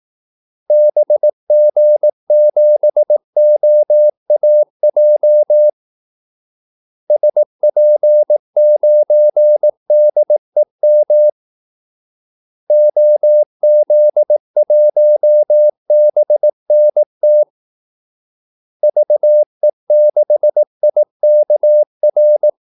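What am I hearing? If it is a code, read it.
Morse code, heard as a keyed tone, about 18 wpm, 600 Hz. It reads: BG7OAJ SP9DEM OZ1BNT VE6IKR
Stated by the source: Morse call-sign practice at 18 wpm.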